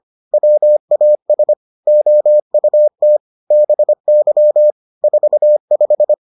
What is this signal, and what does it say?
Morse code, heard as a keyed tone, about 25 wpm, 600 Hz. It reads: WAS OUT BY 45